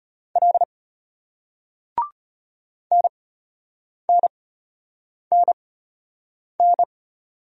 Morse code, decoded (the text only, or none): L NNNN